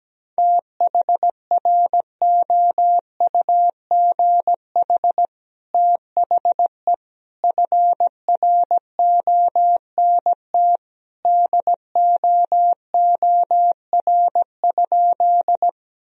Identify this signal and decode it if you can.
Morse code, heard as a keyed tone, about 17 wpm, 700 Hz. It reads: THROUGH THE FRONT DOOR?
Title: Sentences from Top 1000 Words - Review Part 2 - 17wpm